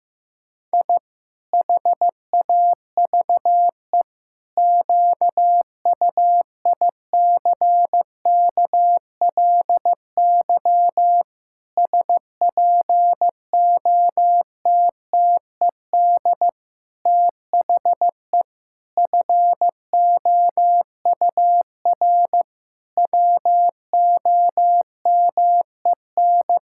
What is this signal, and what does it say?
Morse code, heard as a keyed tone, about 15 wpm, 700 Hz. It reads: I HAVE QUICKLY SPOTTED THE FOUR WOMEN
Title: Pangram Challenge v2 - 15wpm